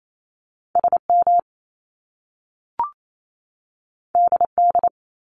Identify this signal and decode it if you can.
Morse code, heard as a keyed tone, about 28 wpm, 700 Hz. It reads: SM DD
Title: Top Two Letter Combinations in Words 28wpm